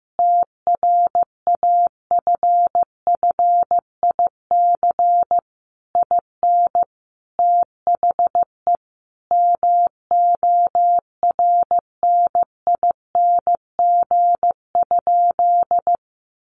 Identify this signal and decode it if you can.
Morse code, heard as a keyed tone, about 15 wpm, 700 Hz. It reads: TRAFFIC IN THE MORNING?
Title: Sentences from Top 1300 Words 15wpm